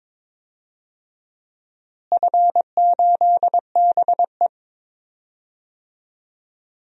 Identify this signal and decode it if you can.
Morse code, heard as a keyed tone, about 22 wpm, 700 Hz. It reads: F8BE
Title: DXCC-100 Call Signs - 4 Characters - Repeated 3x - 22wpm